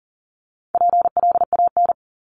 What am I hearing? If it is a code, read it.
Morse code, heard as a keyed tone, about 40 wpm, 700 Hz. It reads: PLAN
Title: ING Suffix - Top 100 Words with Base Words 40wpm